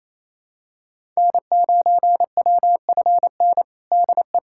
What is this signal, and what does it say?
Morse code, heard as a keyed tone, about 28 wpm, 700 Hz. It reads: N9WFN DE
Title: USA Callsign DE Callsign - 5 Characters - 28wpm